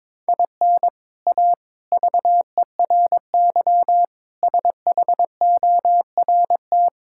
Morse code, heard as a keyed tone, about 22 wpm, 700 Hz. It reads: IN A VERY SHORT
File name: practice / 22wpm / Sets of 7 Words - 22wpm